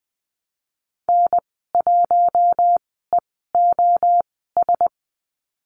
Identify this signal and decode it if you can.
Morse code, heard as a keyed tone, about 20 wpm, spaced as for 14 wpm, 700 Hz. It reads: N1EOS
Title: 076 Lesson 13g - Letter L - Call Signs 20wpm